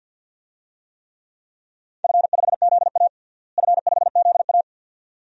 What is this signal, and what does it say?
Morse code, heard as a keyed tone, about 52 wpm, 700 Hz. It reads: F5ZA F5ZA
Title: DXCC-100 Call Signs - 4 Characters - Repeated 2x - 50wpm